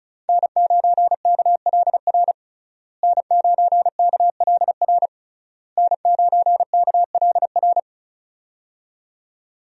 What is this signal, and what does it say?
Morse code, heard as a keyed tone, about 36 wpm, 700 Hz. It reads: N9KLR N9KLR N9KLR
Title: USA Call Signs - 5 Characters - Repeated 3x - 35wpm